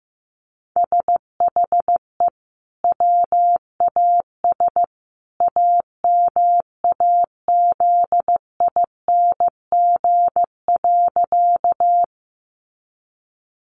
Morse code, heard as a keyed tone, about 15 wpm, 700 Hz. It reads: SHE WAS AMAZING.
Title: Sentences from Top 1500 Words 15wpm